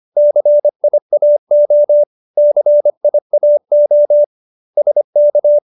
Morse code, heard as a keyed tone, about 25 wpm, 590 Hz. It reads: CIAO CIAO SK